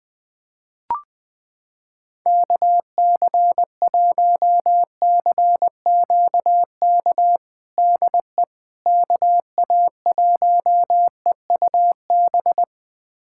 Morse code, heard as a keyed tone, about 21 wpm, 700 Hz. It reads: KC1CQK DE KA1EUB
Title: USA Callsign DE Callsign 20wpm